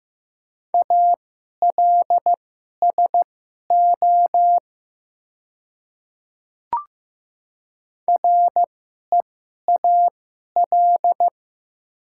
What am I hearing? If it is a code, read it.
Morse code, heard as a keyed tone, about 15 wpm, spaced as for 11 wpm, 700 Hz. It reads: ALSO REAL
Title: 2x Character Spacing - Top 300 Words - 15wpm